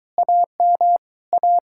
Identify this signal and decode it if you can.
Morse code, heard as a keyed tone, about 23 wpm, 700 Hz. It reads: AM A